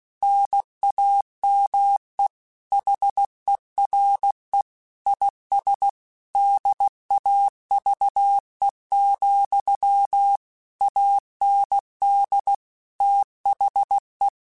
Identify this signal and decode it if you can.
Morse code, heard as a keyed tone, about 16 wpm, 800 Hz. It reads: NAME HERE IS DAVE, AND THE
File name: QSOs / QSOs-mp3s / WA7QOX 2@15